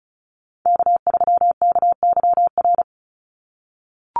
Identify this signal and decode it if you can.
Morse code, heard as a keyed tone, about 34 wpm, 700 Hz. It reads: K3KYR
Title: NA SOTA Chaser Callsigns 35wpm